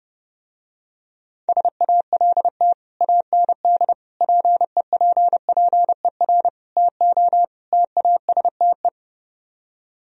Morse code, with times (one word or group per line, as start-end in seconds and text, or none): 1.5-2.7 SALT
3.0-3.9 AND
4.2-6.5 PEPPER
6.8-7.4 TO
7.7-8.9 TASTE